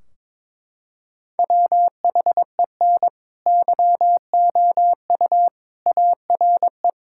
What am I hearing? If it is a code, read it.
Morse code, heard as a keyed tone, about 22 wpm, 700 Hz. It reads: WHEN YOU ARE
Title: Sentences from Top 1600 Words 22wpm